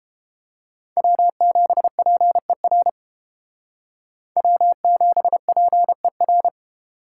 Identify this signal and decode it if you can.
Morse code, heard as a keyed tone, about 31 wpm, 700 Hz. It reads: W7PER W7PER